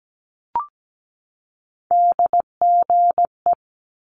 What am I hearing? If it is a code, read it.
Morse code, heard as a keyed tone, about 17 wpm, 700 Hz. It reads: DGE
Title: Multi-letter Phonemes 17wpm